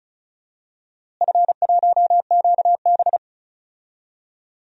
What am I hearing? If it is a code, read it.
Morse code, heard as a keyed tone, about 36 wpm, 700 Hz. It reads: F1QB
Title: DXCC-100 Call Signs - 4 Characters - Repeated 2x - 35wpm